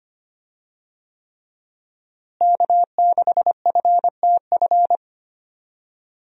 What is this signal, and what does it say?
Morse code, heard as a keyed tone, about 25 wpm, 700 Hz. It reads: K6FTF
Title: USA Call Signs - 5 Characters - Repeated 3x - 25wpm